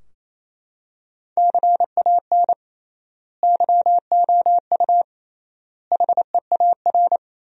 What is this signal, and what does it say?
Morse code, heard as a keyed tone, about 28 wpm, 700 Hz. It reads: CAN YOU HEAR
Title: 3x Word-Spacing - Sentences from Top 300 Words - 28wpm